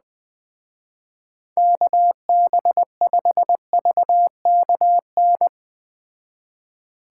Morse code, heard as a keyed tone, about 21 wpm, 700 Hz. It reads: KB5VKN